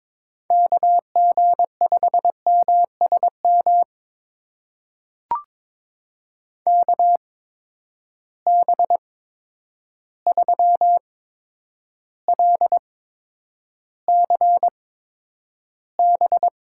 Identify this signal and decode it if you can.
Morse code, heard as a keyed tone, about 21 wpm, spaced as for 9 wpm, 700 Hz. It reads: KG5MSM KB3LCB
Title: USA Call Signs - 6 Characters - 8x Character Spacing - 22wpm